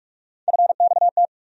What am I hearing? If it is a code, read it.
Morse code, heard as a keyed tone, about 42 wpm, 700 Hz. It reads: FXT
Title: USA Call Sign Postfixes - 3 Letters 45wpm